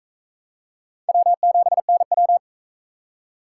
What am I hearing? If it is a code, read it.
Morse code, heard as a keyed tone, about 37 wpm, 700 Hz. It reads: W7NW